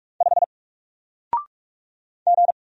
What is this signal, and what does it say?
Morse code, heard as a keyed tone, about 46 wpm, 700 Hz. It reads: 5 G
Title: Single Letter-Number Alt 45wpm